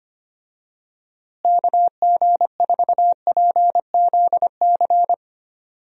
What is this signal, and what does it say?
Morse code, heard as a keyed tone, about 25 wpm, 700 Hz. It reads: KG4PZC
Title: USA Call Signs - 6 Characters - Repeated 3x - 25wpm